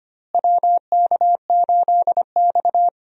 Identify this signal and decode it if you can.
Morse code, heard as a keyed tone, about 25 wpm, 700 Hz. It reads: WK8X
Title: USA Call Signs - 4 Characters - Repeated 3x - 25wpm